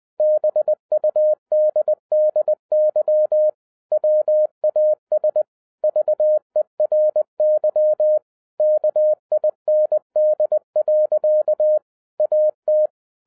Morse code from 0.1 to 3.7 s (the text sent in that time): BUDDY